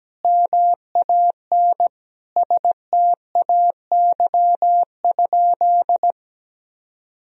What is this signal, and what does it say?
Morse code, heard as a keyed tone, about 17 wpm, 700 Hz. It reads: MAN STAY?
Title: Sentences from Top 1000 Words - Review Part 1 - 17wpm